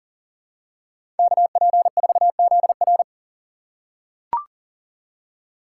Morse code, heard as a keyed tone, about 37 wpm, 700 Hz. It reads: KP4ZR
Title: USA Call Signs - 5 Characters - 4x Character Spacing - 40wpm